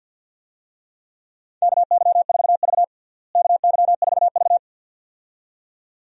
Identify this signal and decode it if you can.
Morse code, heard as a keyed tone, about 50 wpm, 700 Hz. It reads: KY4V KY4V
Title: USA Call Signs - 4 Characters - Repeated 2x - 50wpm